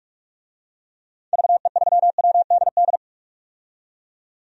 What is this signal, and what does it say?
Morse code, heard as a keyed tone, about 46 wpm, 700 Hz. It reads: VE3WDD
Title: DXCC-100 Call Signs - 6 Characters - 45wpm